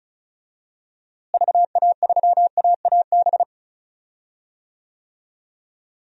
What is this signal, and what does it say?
Morse code, heard as a keyed tone, about 34 wpm, 700 Hz. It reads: VA3AAB
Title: DXCC-100 Call Signs - 6 Characters - Repeated 3x - 35wpm